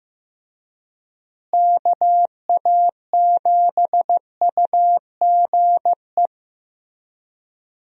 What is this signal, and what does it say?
Morse code, heard as a keyed tone, about 15 wpm, 700 Hz. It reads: KA7UGE
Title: USA Call Signs - 6 Characters - Repeated 3x - 15wpm